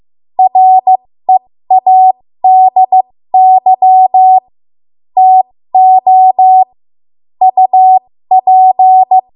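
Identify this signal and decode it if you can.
Morse code, heard as a keyed tone, about 15 wpm, spaced as for 13 wpm, 750 Hz. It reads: READY TO UP
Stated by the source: 13 wpm Morse code practice archive transmission.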